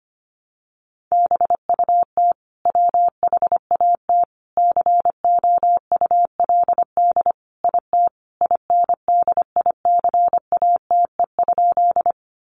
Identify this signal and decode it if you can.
Morse code, heard as a keyed tone, about 25 wpm, 700 Hz. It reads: BUT WHAT COULD IT INDICATE?
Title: Sentences from Top 1300 Words 25wpm